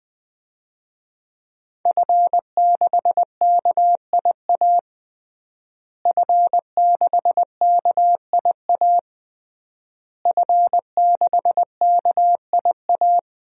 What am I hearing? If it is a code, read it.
Morse code, heard as a keyed tone, about 20 wpm, 700 Hz. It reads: F6KIA F6KIA F6KIA
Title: DXCC-100 Call Signs - 5 Characters - Repeated 3x - 20wpm